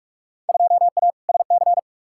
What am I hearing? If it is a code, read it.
Morse code, heard as a keyed tone, about 46 wpm, 700 Hz. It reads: 2A SC
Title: Field Day Exchanges - 45wpm